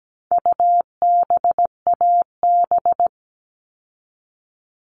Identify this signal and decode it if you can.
Morse code, heard as a keyed tone, about 17 wpm, 700 Hz. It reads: UBAB
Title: All Cut Numbers - 4-Digit 17wpm